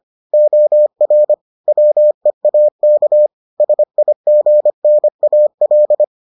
Morse code, heard as a keyed tone, about 25 wpm, 600 Hz. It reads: OR WEAK SIGNAL